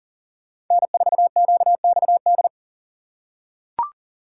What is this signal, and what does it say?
Morse code, heard as a keyed tone, about 40 wpm, 700 Hz. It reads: N4QXD
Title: USA Call Signs - 5 Characters - 3x Character Spacing - 40wpm